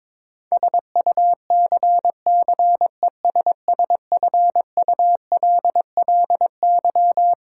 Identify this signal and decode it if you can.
Morse code, heard as a keyed tone, about 21 wpm, 700 Hz. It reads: SUCCESSFULLY
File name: Twelve Letter Words - 22wpm